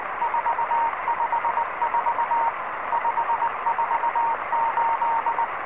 Morse code, heard as a keyed tone, about 20 wpm, 930 Hz, with noise. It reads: 454 548
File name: ao07beacon